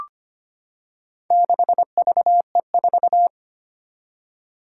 Morse code, heard as a keyed tone, about 25 wpm, 700 Hz. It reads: 6VE4
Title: Four Challenging Letter-Number Combinations 25wpm